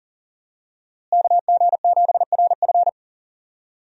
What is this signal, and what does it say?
Morse code, heard as a keyed tone, about 40 wpm, 700 Hz. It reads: KG7RF
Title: USA Call Signs - 5 Characters - Repeated 3x - 40wpm